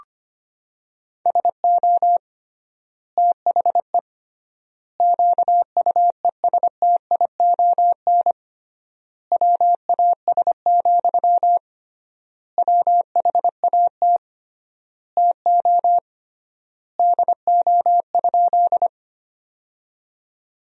Morse code, heard as a keyed tone, about 25 wpm, 700 Hz. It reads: SO THE QUESTION WAS, WHAT TO DO?
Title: 3x Word-Spacing - Sentences from Top 400 Words - 25wpm